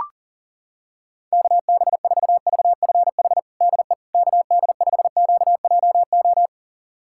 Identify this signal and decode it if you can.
Morse code, heard as a keyed tone, about 40 wpm, 700 Hz. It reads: KB4VFH DE KD5QJO
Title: USA Callsign DE Callsign 40wpm